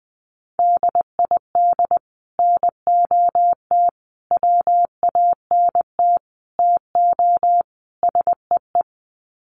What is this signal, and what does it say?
Morse code, heard as a keyed tone, about 20 wpm, 700 Hz. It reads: DID NOT WANT TO SEE
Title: Sets of 5 Words Encore2 20wpm